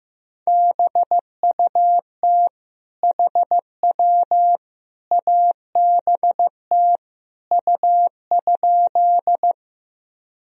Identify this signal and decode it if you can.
Morse code, heard as a keyed tone, about 16 wpm, 700 Hz. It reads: BUT HW ABT U?
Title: Sentences from Top 100 Words and Abbreviations 15wpm